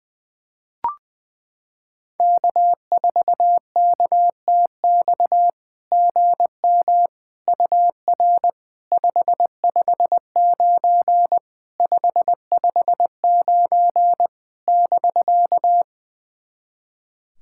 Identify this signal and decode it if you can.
Morse code, heard as a keyed tone, about 21 wpm, 690 Hz. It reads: K4KTX GM UR 559 559 <BK>
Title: SOTA QSO Practice 20wpm